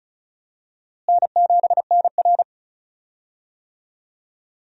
Morse code, heard as a keyed tone, about 34 wpm, 700 Hz. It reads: N7NR
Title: USA Call Signs - 4 Characters - Repeated 2x - 35wpm